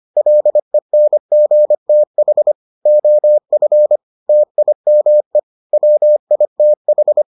Morse code, heard as a keyed tone, about 25 wpm, 600 Hz. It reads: LENGTH OF TIME WITH